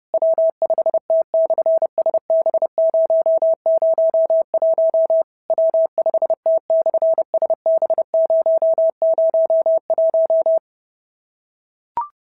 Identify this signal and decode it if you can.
Morse code, heard as a keyed tone, about 30 wpm, 650 Hz. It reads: W5T/SB001 W5T/SB001
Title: SOTA QSO Practice 30wpm